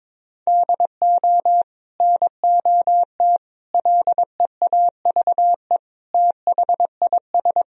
Morse code, heard as a keyed tone, about 22 wpm, 700 Hz. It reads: DO NOT LEAVE THIS
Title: Sentences from Top 1000 Words - Review Part 1 - 22wpm